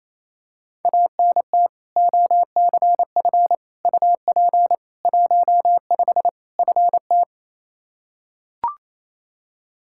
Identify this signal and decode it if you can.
Morse code, heard as a keyed tone, about 27 wpm, 700 Hz. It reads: ANT OCF UP 15 FT
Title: Practice QSO - Exchange 2 - Info - 28wpm